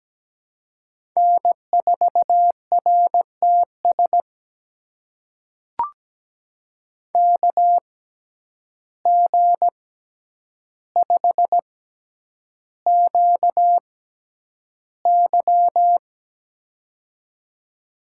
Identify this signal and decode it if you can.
Morse code, heard as a keyed tone, about 17 wpm, spaced as for 9 wpm, 700 Hz. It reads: N4RTS KG5QY
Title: USA Call Signs - 5 Characters - 6x Character Spacing - 17wpm